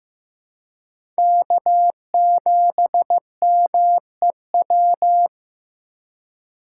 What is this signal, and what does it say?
Morse code, heard as a keyed tone, about 15 wpm, 700 Hz. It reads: K7MEW
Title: USA Call Signs - 5 Characters - 3x Character Spacing - 15wpm